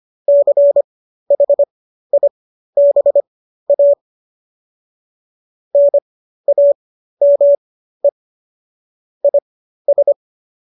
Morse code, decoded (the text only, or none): CHIBA NAME IS